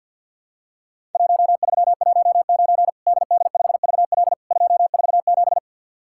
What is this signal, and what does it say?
Morse code, heard as a keyed tone, about 50 wpm, 700 Hz. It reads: 1319 DD5VL 247
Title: Worked All Europe DX Contest - QTC Exchange - 50wpm